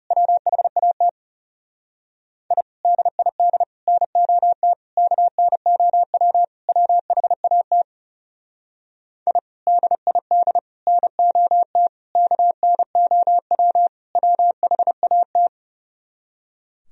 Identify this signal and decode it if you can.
Morse code, heard as a keyed tone, about 36 wpm, 700 Hz. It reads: WHAT I DID NOT KNOW WHAT I DID NOT KNOW WHAT
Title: Speed-Racing - Sets of 5 Words 30wpm